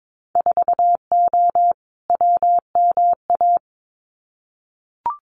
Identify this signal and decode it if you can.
Morse code, heard as a keyed tone, about 22 wpm, 700 Hz. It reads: 4O WMA
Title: Winter Field Day Exchanges Encore 22wpm